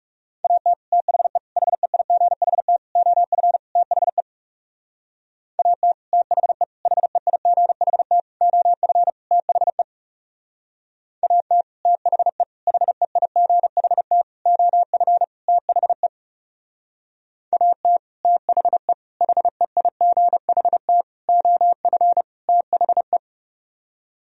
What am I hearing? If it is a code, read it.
Morse code, heard as a keyed tone, about 37 wpm, 700 Hz. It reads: AT THE HEIGHT OF THE AT THE HEIGHT OF THE AT THE HEIGHT OF THE AT THE HEIGHT OF THE
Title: Speed-Racing - Sets of 5 Words 30wpm